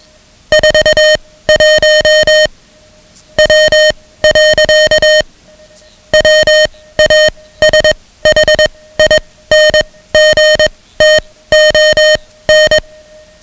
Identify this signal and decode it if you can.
Morse code, heard as a keyed tone, about 21 wpm, spaced as for 16 wpm, 620 Hz, with noise. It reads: 41 W. WASHINGTON
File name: ham radio / CodeQuick / Code Quick 10